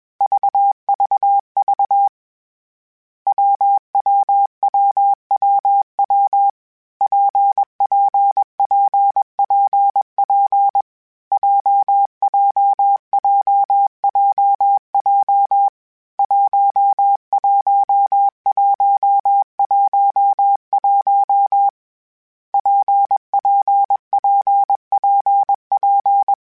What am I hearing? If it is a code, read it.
Morse code, heard as a keyed tone, about 21 wpm, 800 Hz. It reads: VVV WWWWW PPPPP JJJJJ 11111 PPPPP
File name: Variable Speed / Variable Speed-mp3s / WPJ WPJ1 sp @20